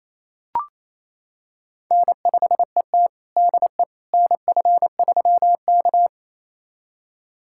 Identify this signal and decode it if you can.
Morse code, heard as a keyed tone, about 28 wpm, 700 Hz. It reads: N5ET DE NF3K
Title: USA Callsign DE Callsign - 4 Characters - 28wpm